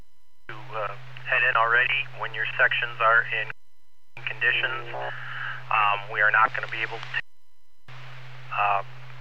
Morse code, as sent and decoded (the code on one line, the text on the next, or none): none